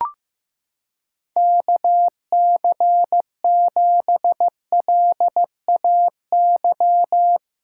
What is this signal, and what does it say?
Morse code, heard as a keyed tone, about 15 wpm, 700 Hz. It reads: KC7LAY
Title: USA Callsign DE Callsign - 6 Characters - 15wpm